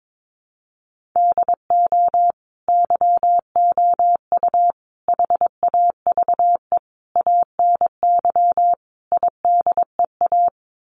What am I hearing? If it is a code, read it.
Morse code, heard as a keyed tone, about 22 wpm, 700 Hz. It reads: DO YOU HAVE ANY IDEA
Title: Sets of 5 Words Encore 22wpm